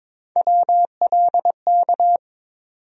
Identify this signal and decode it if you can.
Morse code, heard as a keyed tone, about 22 wpm, 700 Hz. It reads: WLK
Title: USA Call Sign Postfixes - 3 Letters 22wpm